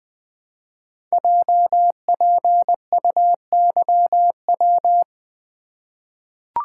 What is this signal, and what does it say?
Morse code, heard as a keyed tone, about 20 wpm, 700 Hz. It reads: JPUYW
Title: Code Groups - 5 Letters - 20wpm